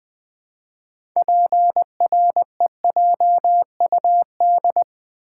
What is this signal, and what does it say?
Morse code, heard as a keyed tone, about 20 wpm, 700 Hz. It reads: PREJUD
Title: Build A Word - 3 Letters At A Time - 9-Letter Words - 20wpm